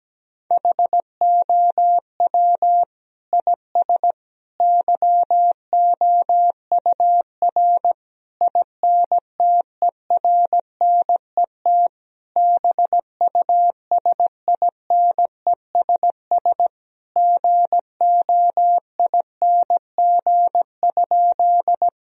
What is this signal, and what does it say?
Morse code, heard as a keyed tone, about 17 wpm, 700 Hz. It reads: HOW IS YOUR INTERNET BUSINESS GOING?
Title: Sentences from Top 1000 Words - Review Part 4 - 17wpm